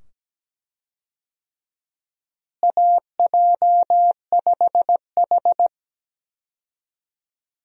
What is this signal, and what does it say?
Morse code, heard as a keyed tone, about 17 wpm, 700 Hz. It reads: AJ5H
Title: USA Call Signs - 4 Characters - Repeated 2x - 17wpm